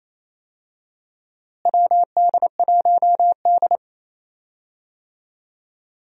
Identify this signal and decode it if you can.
Morse code, heard as a keyed tone, about 28 wpm, 700 Hz. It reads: WD1D
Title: USA Call Signs - 4 Characters - Repeated 3x - 28wpm